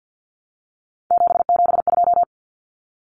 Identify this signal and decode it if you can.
Morse code, heard as a keyed tone, about 50 wpm, 700 Hz. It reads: 772
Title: Numbers - 3-Digit 50wpm